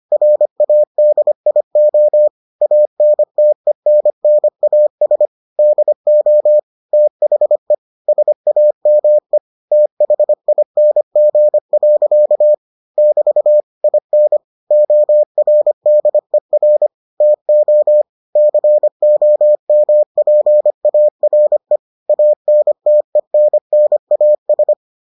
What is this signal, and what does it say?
Morse code, heard as a keyed tone, about 25 wpm, 600 Hz. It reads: RADIO ANTENNAS DO THE SAME THING. <BT> IN ORDER TO COMPARE ANTENNAS